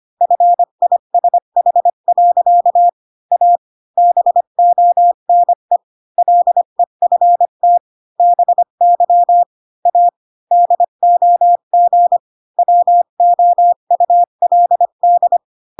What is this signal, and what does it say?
Morse code, heard as a keyed tone, about 25 wpm, 700 Hz. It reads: FISH. A BONE LEFT BY A DOG WOULD
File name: Pinocchio / Pinocchio ch5 20 25 ik1hge